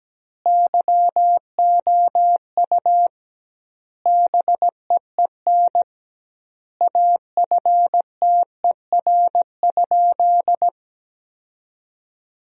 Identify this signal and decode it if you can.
Morse code, heard as a keyed tone, about 17 wpm, 700 Hz. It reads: YOU BEEN AFTER?
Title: 2x Word-Spacing - Sentences from Top 200 Words - 17wpm